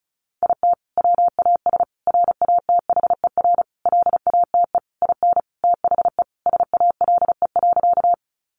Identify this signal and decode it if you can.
Morse code, heard as a keyed tone, about 34 wpm, 700 Hz. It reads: IT WAS RATHER LATE IN THE SALE.
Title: Sentences from Top 1400 Words 35wpm